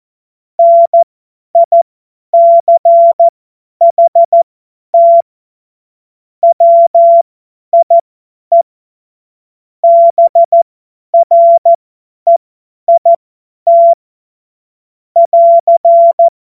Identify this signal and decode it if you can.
Morse code, heard as a keyed tone, about 14 wpm, spaced as for 10 wpm, 680 Hz. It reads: NICHT WIE BREIT <AR>
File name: Morsekurs / Witze / Witz 28. Dezember 2021, 50 F70